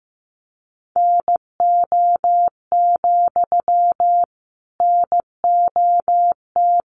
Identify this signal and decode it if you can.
Morse code, heard as a keyed tone, about 15 wpm, 700 Hz. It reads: NO, NOT
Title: Sentences from Top 1000 Words 15wpm